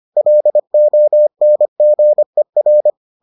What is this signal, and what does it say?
Morse code, heard as a keyed tone, about 25 wpm, 600 Hz. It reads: LONGER